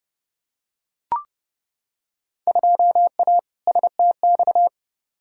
Morse code, heard as a keyed tone, about 31 wpm, 700 Hz. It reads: 2A STX
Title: Field Day Exchanges - 30wpm